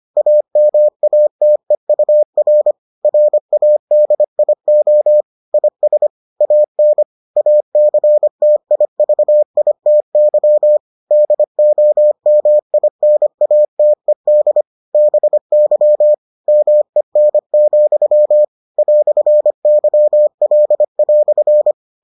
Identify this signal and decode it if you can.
Morse code, heard as a keyed tone, about 25 wpm, 600 Hz. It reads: AMATEUR RADIO IS AN ACTIVITY DOMINATED BY MEN, "YL"